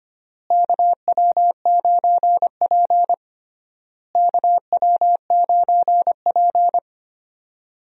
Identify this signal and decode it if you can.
Morse code, heard as a keyed tone, about 25 wpm, 700 Hz. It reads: KW9P KW9P
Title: USA Call Signs - 4 Characters - Repeated 3x - 25wpm